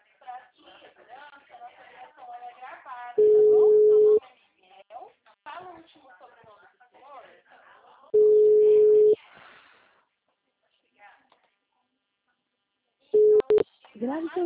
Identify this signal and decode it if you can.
Ringback tone (Europe, 425 Hz), with speech.